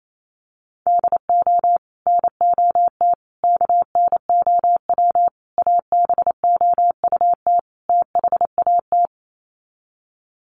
Morse code, heard as a keyed tone, about 28 wpm, 700 Hz. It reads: DO NOT KNOW ABOUT THAT
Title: Sets of 5 Words Encore2 28wpm